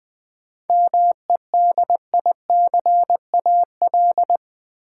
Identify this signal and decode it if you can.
Morse code, heard as a keyed tone, about 20 wpm, 700 Hz. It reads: MEDICAL
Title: Sentences from Top 800 Words - New Word Drill - 20wpm